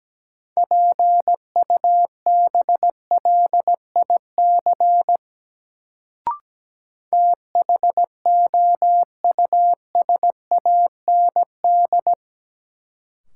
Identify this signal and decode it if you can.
Morse code, heard as a keyed tone, about 17 wpm, 700 Hz. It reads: PUBLIC THOUSAND
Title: Sentences from Top 600 Words - New Word Drill - 17wpm